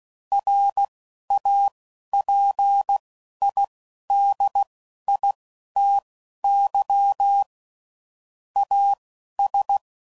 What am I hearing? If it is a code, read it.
Morse code, heard as a keyed tone, about 16 wpm, spaced as for 11 wpm, 800 Hz. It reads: RAPIDITY AS
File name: CW Code Groups @10WPM / Technical Subjects / XC